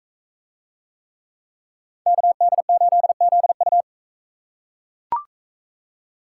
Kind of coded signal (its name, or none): Morse code